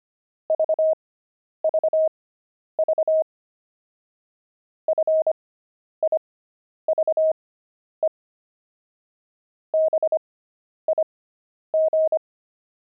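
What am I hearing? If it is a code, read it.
Morse code, heard as a keyed tone, about 25 wpm, spaced as for 10 wpm, 630 Hz. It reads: VVV FIVE BIG
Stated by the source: Pangram Morse practice at 25 wpm character speed with 10 wpm Farnsworth spacing, intermediate level.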